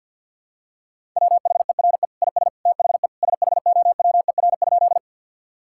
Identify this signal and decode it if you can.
Morse code, heard as a keyed tone, about 50 wpm, 700 Hz. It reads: WHERE IS THE SHOWER?
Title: Sentences from Top 1600 Words 50wpm